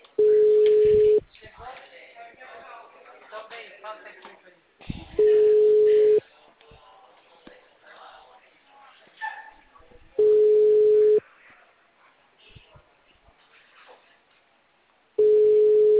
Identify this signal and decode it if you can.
Ringback tone (Europe, 425 Hz), with speech.
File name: out- -2015